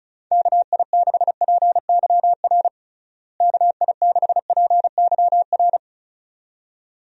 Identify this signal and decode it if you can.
Morse code, heard as a keyed tone, about 36 wpm, 700 Hz. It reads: KI6PYR KI6PYR
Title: USA Call Signs - 6 Characters - Repeated 2x - 35wpm